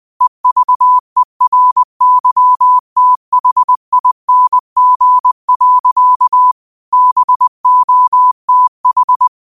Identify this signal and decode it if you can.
Morse code, heard as a keyed tone, about 20 wpm, 1000 Hz. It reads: EVERYTHING. BOTH
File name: qotdfiles / 20wpm